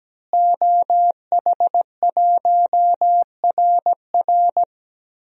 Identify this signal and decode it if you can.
Morse code, heard as a keyed tone, about 17 wpm, 700 Hz. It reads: OH1RR
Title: DXCC-100 Call Signs - 5 Characters - Repeated 3x - 17wpm